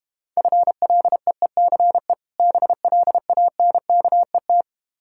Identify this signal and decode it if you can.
Morse code, heard as a keyed tone, about 32 wpm, 700 Hz. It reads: FLEECE BLANKET